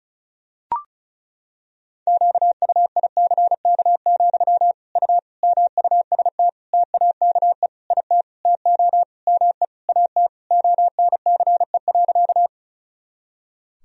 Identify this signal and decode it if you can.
Morse code, heard as a keyed tone, about 34 wpm, 700 Hz. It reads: QUICK, U MUST TAKE IT TO ME AT ONCE.
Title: Sentences from Top 500 Words and Abbreviations 35wpm